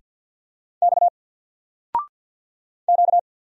Morse code, heard as a keyed tone, about 50 wpm, 700 Hz. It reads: X Q
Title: Single Letter-Number 50wpm